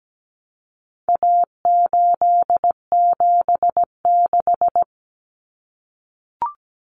Morse code, heard as a keyed tone, about 17 wpm, 700 Hz. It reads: A876
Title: Common Cut Numbers - 4-Digit 17wpm